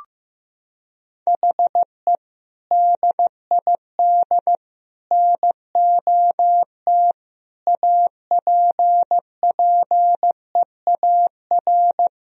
Morse code, heard as a keyed tone, about 15 wpm, 700 Hz. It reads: HE DID NOT APPEAR
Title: Sentences from Top 1000 Words - Review Part 2 - 15wpm